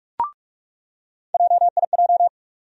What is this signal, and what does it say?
Morse code, heard as a keyed tone, about 42 wpm, 700 Hz. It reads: JIJ
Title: USA Call Sign Postfixes - 3 Letters 45wpm